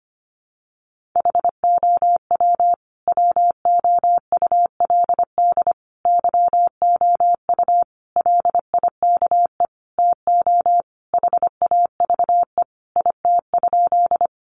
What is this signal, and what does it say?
Morse code, heard as a keyed tone, about 25 wpm, 700 Hz. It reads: HOW WOULD YOU LIKE TO HAVE IT?
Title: Sentences from Top 100 Words 25wpm